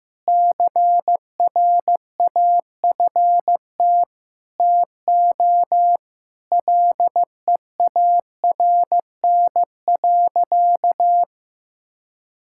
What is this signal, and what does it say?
Morse code, heard as a keyed tone, about 15 wpm, 700 Hz. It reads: CRAFT TO LEARN.
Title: Sentences from Top 1000 Words - Review Part 6 - 15wpm